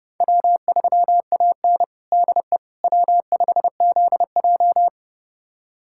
Morse code, heard as a keyed tone, about 30 wpm, 700 Hz. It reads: W3AN DE W5ZJ
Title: USA Callsign DE Callsign - 4 Characters - 30wpm